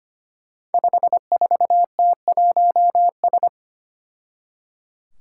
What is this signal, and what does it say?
Morse code, heard as a keyed tone, about 25 wpm, 700 Hz. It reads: 54T1S
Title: Five Challenging Letter-Number Combinations 25wpm